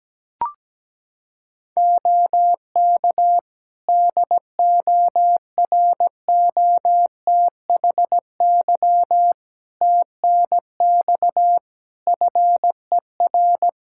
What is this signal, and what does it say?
Morse code, heard as a keyed tone, about 17 wpm, 700 Hz. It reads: OK DOROTHY TNX FER
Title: Practice QSO - Exchange 4 - Ending - 17wpm